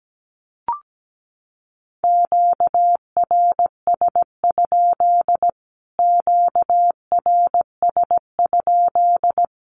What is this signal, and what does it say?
Morse code, heard as a keyed tone, about 17 wpm, 700 Hz. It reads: QRS? QRS?
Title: Q-Codes Repeated 3x 17wpm